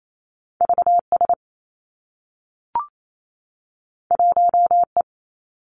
Morse code, heard as a keyed tone, about 28 wpm, 700 Hz. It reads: V S 1 E
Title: Two Challenging Letter-Number Combinations 28wpm